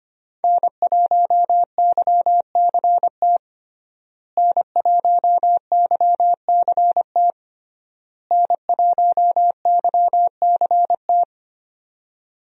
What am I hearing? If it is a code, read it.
Morse code, heard as a keyed tone, about 25 wpm, 700 Hz. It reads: N1YCT N1YCT N1YCT
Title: USA Call Signs - 5 Characters - Repeated 3x - 25wpm